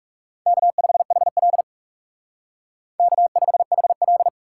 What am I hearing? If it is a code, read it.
Morse code, heard as a keyed tone, about 37 wpm, 700 Hz. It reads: K5HL K5HL
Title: Speed Racing - USA 1x2 Call Signs 30wpm